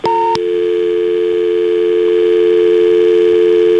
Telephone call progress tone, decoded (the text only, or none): dial tone